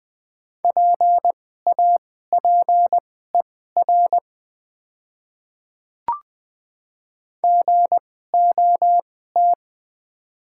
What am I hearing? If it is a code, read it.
Morse code, heard as a keyed tone, about 20 wpm, spaced as for 15 wpm, 700 Hz. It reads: PAPER GOT
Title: 2x Character Spacing - Top 300 Words - 20wpm